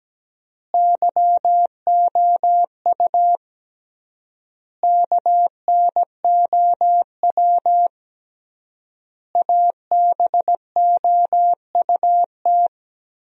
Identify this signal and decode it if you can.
Morse code, heard as a keyed tone, about 17 wpm, 700 Hz. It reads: YOU KNOW ABOUT
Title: 3x Word-Spacing - Sentences from Top 100 Words - 17wpm